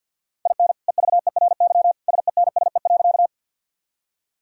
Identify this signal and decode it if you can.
Morse code, heard as a keyed tone, about 52 wpm, 700 Hz. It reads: IN EVERY SENSE.
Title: Sentences from Top 1000 Words - Review Part 4 - 50wpm